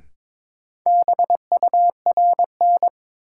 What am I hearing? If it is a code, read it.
Morse code, heard as a keyed tone, about 22 wpm, 700 Hz. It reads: BURN
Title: Sentences from Top 1300 Words - New Word Drill - 22wpm